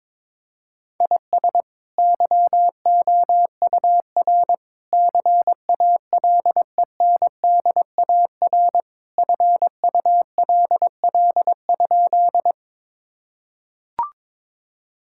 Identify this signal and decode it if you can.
Morse code, heard as a keyed tone, about 21 wpm, 700 Hz. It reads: IS YOUR CALENDAR FULL?
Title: Sentences from Top 1600 Words 22wpm